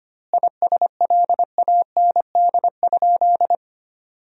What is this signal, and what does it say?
Morse code, heard as a keyed tone, about 25 wpm, 700 Hz. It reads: ISLAND?
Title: 2x Word-Spacing - Sentences from Top 500 Words - 25wpm